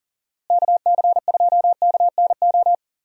Morse code, heard as a keyed tone, about 40 wpm, 700 Hz. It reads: KC2KNO
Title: USA Call Signs - 6 Characters - Repeated 2x - 40wpm